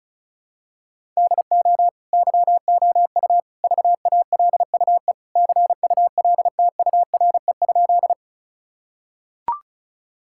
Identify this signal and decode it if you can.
Morse code, heard as a keyed tone, about 36 wpm, 700 Hz. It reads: DO YOU VALUE CULTURE?
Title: Sentences from Top 1000 Words - Review Part 6 - 35wpm